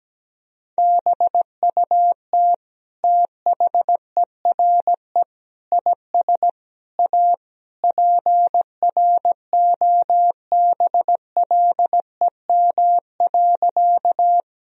Morse code, heard as a keyed tone, about 17 wpm, 700 Hz. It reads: BUT THERE IS A PROBLEM.
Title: Sentences from Top 1000 Words - Review Part 2 - 17wpm